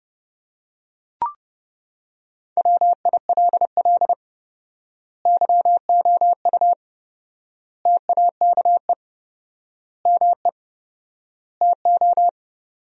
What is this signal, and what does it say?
Morse code, heard as a keyed tone, about 30 wpm, 700 Hz. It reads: WILL YOU TAKE ME TO
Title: 4x Word-Spacing - Sentences from Top 200 Words - 30wpm